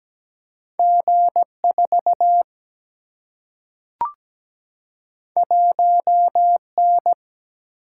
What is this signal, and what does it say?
Morse code, heard as a keyed tone, about 17 wpm, 700 Hz. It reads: G4 1N
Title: Single-Letter Single-Number 17wpm